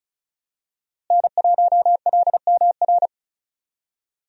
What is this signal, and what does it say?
Morse code, heard as a keyed tone, about 34 wpm, 700 Hz. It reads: N1LMR
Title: USA Call Signs - 5 Characters - Repeated 2x - 35wpm